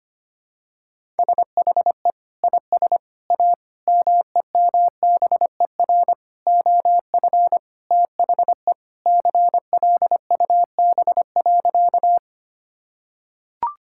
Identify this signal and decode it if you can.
Morse code, heard as a keyed tone, about 25 wpm, 700 Hz. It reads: SHE IS A MEMBER OF THE CLUB.
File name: practice / 25wpm / Sentences with 4-Word Phrases - 25wpm